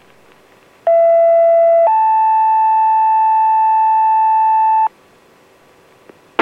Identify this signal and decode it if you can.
Two-tone page, heard as a one-second tone, then a three-second tone, about 670 then 880 Hz. It is heard from 0.9 to 4.9 s.